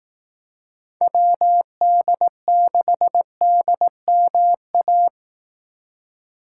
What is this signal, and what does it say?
Morse code, heard as a keyed tone, about 18 wpm, 700 Hz. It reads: WD6DMA
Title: Speed Racing - USA 2x3 Call Signs 15wpm